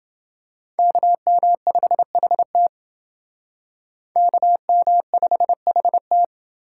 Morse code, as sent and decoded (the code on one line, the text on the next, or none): -.- -- ..... .... - / -.- -- ..... .... -
KM5HT KM5HT